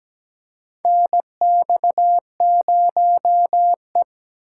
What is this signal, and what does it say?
Morse code, heard as a keyed tone, about 17 wpm, 700 Hz. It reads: NX0E
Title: USA Call Signs - 4 Characters - 3x Character Spacing - 17wpm